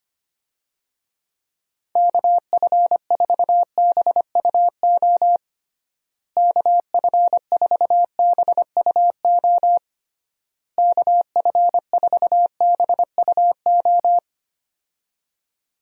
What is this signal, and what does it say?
Morse code, heard as a keyed tone, about 25 wpm, 700 Hz. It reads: KF4BUO KF4BUO KF4BUO
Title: USA Call Signs - 6 Characters - Repeated 3x - 25wpm